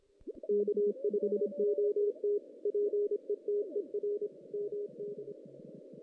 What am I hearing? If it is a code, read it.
Morse code, heard as a keyed tone, about 25 wpm, 410 Hz. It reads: K5OT PETER MD
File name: 2016 JAN CW NAQP